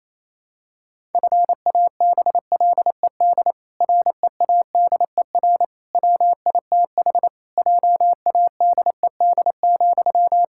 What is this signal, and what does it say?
Morse code, heard as a keyed tone, about 28 wpm, 700 Hz. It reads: FABLED READER WITH JADED,
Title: Pangram Challenge v2 - 28wpm